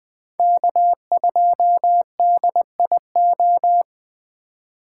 Morse code, heard as a keyed tone, about 20 wpm, 700 Hz. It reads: K2DIO